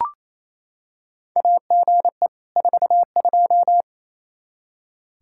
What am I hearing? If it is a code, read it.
Morse code, heard as a keyed tone, about 28 wpm, 700 Hz. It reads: AGE 42
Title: Practice QSO - Exchange 3 - More Info - 28wpm